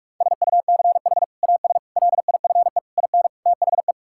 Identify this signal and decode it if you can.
Morse code, heard as a keyed tone, about 41 wpm, 700 Hz. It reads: SUCH AS LIFE IN THE
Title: Sentences from Top 1000 Words - Review Part 5 - 45wpm